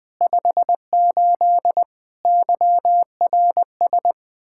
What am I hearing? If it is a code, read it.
Morse code, heard as a keyed tone, about 20 wpm, 700 Hz. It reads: 58 YRS
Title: Practice QSO - Exchange 3 - More Info - 20wpm